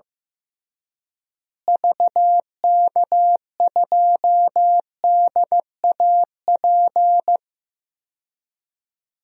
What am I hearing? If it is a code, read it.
Morse code, heard as a keyed tone, about 15 wpm, 700 Hz. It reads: VK2DAP
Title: DXCC-100 Call Signs - 6 Characters - Repeated 3x - 15wpm